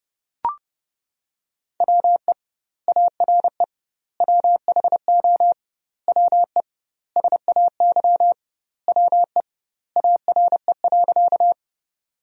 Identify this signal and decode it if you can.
Morse code, heard as a keyed tone, about 30 wpm, 700 Hz. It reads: WE ARE WHO WE SAY WE ARE.
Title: 2x Word-Spacing - Sentences from Top 200 Words - 30wpm